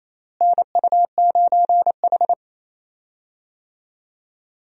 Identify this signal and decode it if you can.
Morse code, heard as a keyed tone, about 28 wpm, 700 Hz. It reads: NU9H
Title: USA Call Signs - 4 Characters - Repeated 3x - 28wpm